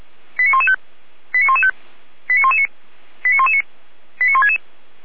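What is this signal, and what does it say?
Five-tone selective call (ZVEI), heard as quick runs of five tones, tones of about 70 ms. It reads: 89115 89115 89119 89119 89166
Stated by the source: ZVEI five-tone selective call for address 89067, received by SDR, from the feuerwehr-orlamuende (Orlamünde fire brigade) audio server.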